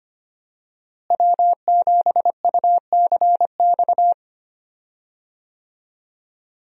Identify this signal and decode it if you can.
Morse code, heard as a keyed tone, about 25 wpm, 700 Hz. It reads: W7UCX
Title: USA Call Signs - 5 Characters - Repeated 3x - 25wpm